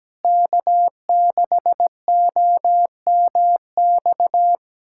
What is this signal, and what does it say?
Morse code, heard as a keyed tone, about 17 wpm, 700 Hz. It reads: K6OMX
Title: USA Callsign DE Callsign - 5 Characters - 17wpm